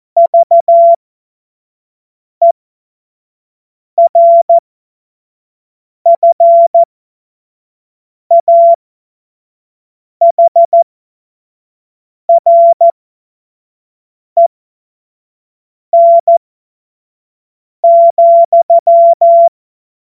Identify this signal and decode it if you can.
Morse code, heard as a keyed tone, about 14 wpm, spaced as for 5 wpm, 680 Hz. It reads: VERFAHREN,
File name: Morsekurs / Witze / Witz 06. April 2021, 25 F70